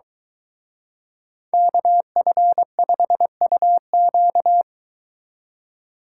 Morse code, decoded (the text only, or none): KF5UQ